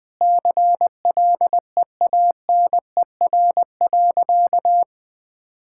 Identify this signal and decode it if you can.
Morse code, heard as a keyed tone, about 21 wpm, 700 Hz. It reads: CLEANER.
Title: Pangram Challenge v2 - 20wpm